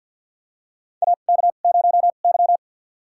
Morse code, heard as a keyed tone, about 50 wpm, spaced as for 36 wpm, 700 Hz. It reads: AK0Y
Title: USA Call Signs - 4 Characters - 2x Character Spacing - 50wpm